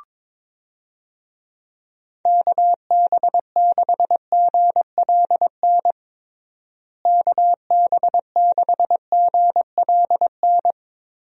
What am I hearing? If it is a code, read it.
Morse code, heard as a keyed tone, about 22 wpm, 700 Hz. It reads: KB6GLN KB6GLN
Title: USA Call Signs - 6 Characters - Repeated 3x - 22wpm